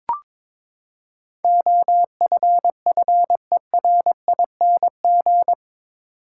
Morse code, heard as a keyed tone, about 22 wpm, 700 Hz. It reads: OFFERING
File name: practice / 22wpm / Eight Letter Words - 22wpm